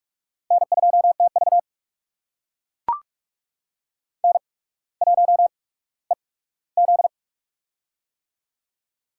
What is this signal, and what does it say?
Morse code, heard as a keyed tone, about 44 wpm, spaced as for 19 wpm, 700 Hz. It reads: N2TV N1EZ
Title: USA Call Signs - 8x Character Spacing - 45wpm